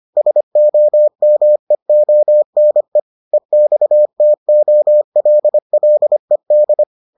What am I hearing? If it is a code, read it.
Morse code, heard as a keyed tone, about 25 wpm, 600 Hz. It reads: SOMEONE EXTOLLED